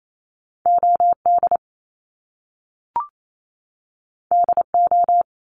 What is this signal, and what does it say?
Morse code, heard as a keyed tone, about 28 wpm, 700 Hz. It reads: OD DO
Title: Top Two Letter Combinations in Words 28wpm